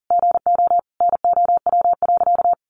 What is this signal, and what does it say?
Morse code, heard as a keyed tone, about 40 wpm, 700 Hz. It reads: GO NOW.
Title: Sentences from Top 1300 Words 40wpm